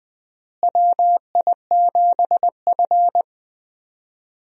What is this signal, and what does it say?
Morse code, heard as a keyed tone, about 20 wpm, 700 Hz. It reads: WI7F
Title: USA Call Signs - 4 Characters - Repeated 3x - 20wpm